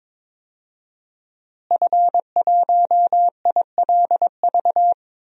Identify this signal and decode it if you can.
Morse code, heard as a keyed tone, about 22 wpm, 700 Hz. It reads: F1ILV
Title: DXCC-100 Call Signs - 5 Characters - Repeated 3x - 22wpm